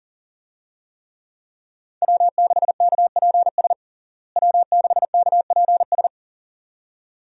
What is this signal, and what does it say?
Morse code, heard as a keyed tone, about 37 wpm, 700 Hz. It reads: W6KPS W6KPS
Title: USA Call Signs - 5 Characters - Repeated 2x - 40wpm